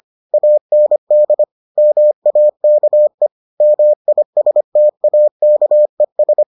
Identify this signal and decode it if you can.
Morse code, heard as a keyed tone, about 25 wpm, 600 Hz. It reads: AND MAKE MISTAKES